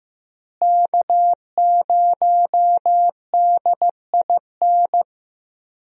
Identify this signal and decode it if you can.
Morse code, heard as a keyed tone, about 15 wpm, 700 Hz. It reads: K0DIN
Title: USA Call Signs - 5 Characters - Repeated 3x - 15wpm